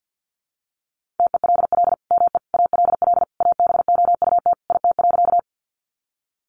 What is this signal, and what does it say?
Morse code, heard as a keyed tone, about 50 wpm, 700 Hz. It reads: TELL ME ALL ABOUT IT.
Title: Sentences from Top 200 Words 50wpm